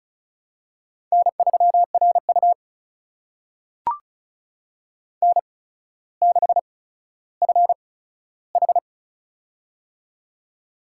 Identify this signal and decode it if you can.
Morse code, heard as a keyed tone, about 35 wpm, spaced as for 15 wpm, 700 Hz. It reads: N3RU N6FH
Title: USA Call Signs - 4 Characters - 8x Character Spacing - 35wpm